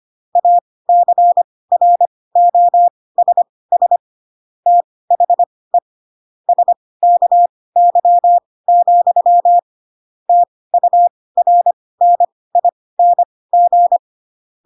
Morse code, heard as a keyed tone, about 25 wpm, spaced as for 18 wpm, 700 Hz. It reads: ACROSS THE SKY, TURNING
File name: Pinocchio / Pinocchio ch6 12 25 ik1hge